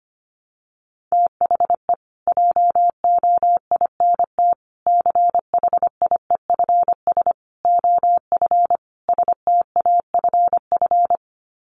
Morse code, heard as a keyed tone, about 25 wpm, 700 Hz. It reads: THE JOINT CHIEFS OF STAFF
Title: Sets of 5 Words Encore2 25wpm